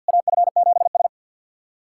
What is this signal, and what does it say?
Morse code, heard as a keyed tone, about 50 wpm, 700 Hz. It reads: AF7S
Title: SOTA QSO Practice 50wpm